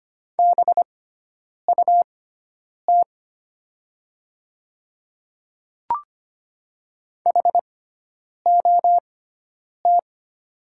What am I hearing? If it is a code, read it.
Morse code, heard as a keyed tone, about 25 wpm, spaced as for 9 wpm, 700 Hz. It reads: BUT HOT
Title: 6x Character Spacing - Top 200 Words - 25wpm